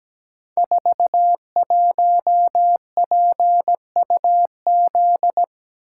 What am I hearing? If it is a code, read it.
Morse code, heard as a keyed tone, about 17 wpm, 700 Hz. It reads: 41PUZ
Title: Code Groups - 5 Letter:Numbers - 17wpm